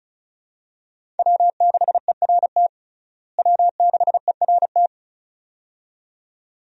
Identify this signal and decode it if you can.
Morse code, heard as a keyed tone, about 36 wpm, 700 Hz. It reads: W6ERT W6ERT
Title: USA Call Signs - 5 Characters - Repeated 2x - 35wpm